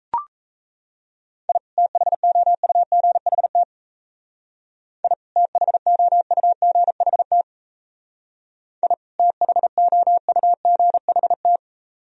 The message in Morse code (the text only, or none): I THOUGHT I THOUGHT I THOUGHT